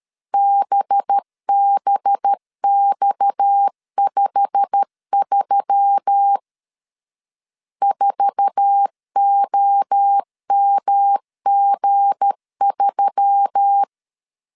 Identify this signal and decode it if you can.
Morse code, heard as a keyed tone, about 13 wpm, 800 Hz. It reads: BBX53 4OMG3
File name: LEZIONI / 10 SIGNS / 10D COMA 12wpm 2544